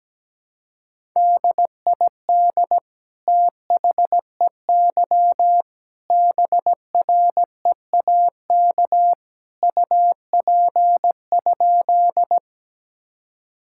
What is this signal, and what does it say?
Morse code, heard as a keyed tone, about 17 wpm, 700 Hz. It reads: DID THEY BREAK UP?
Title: Sentences from Top 1000 Words - Review Part 6 - 17wpm